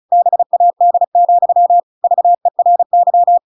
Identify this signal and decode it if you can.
Morse code, heard as a keyed tone, about 35 wpm, 700 Hz. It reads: BAD, VERY
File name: Pinocchio / Pinocchio ch7 35 35 ik1hge